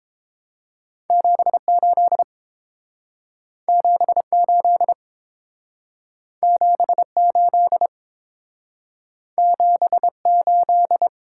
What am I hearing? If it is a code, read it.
Morse code, heard as a keyed tone, about 31 wpm, 700 Hz. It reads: 78 78 78 78